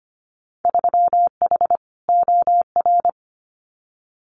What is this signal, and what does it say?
Morse code, heard as a keyed tone, about 25 wpm, 700 Hz. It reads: 3H OR
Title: Winter Field Day Exchanges Encore 25wpm